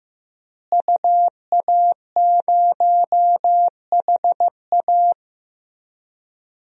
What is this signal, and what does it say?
Morse code, heard as a keyed tone, about 15 wpm, 700 Hz. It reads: UA0HA
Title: DXCC-100 Call Signs - 5 Characters - Repeated 3x - 15wpm